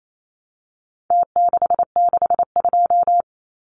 Morse code, heard as a keyed tone, about 28 wpm, 700 Hz. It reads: T662
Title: Numbers - Contest Style - 4-Digit 28wpm